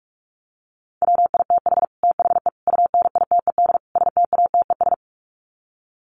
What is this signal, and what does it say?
Morse code, heard as a keyed tone, about 44 wpm, 700 Hz. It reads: WITH THE UNITED STATES
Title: Sets of 4 Words 45wpm